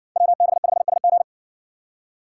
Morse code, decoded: WB5SG

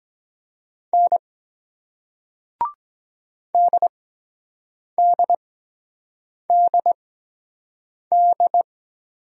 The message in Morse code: N DDDD

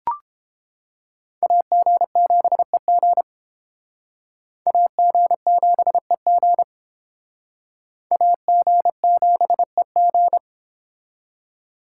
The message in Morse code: AG7EG AG7EG AG7EG

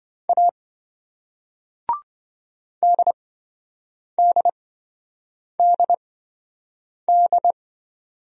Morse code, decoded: A DDDD

